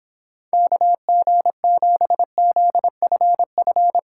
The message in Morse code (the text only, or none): KG7ZFF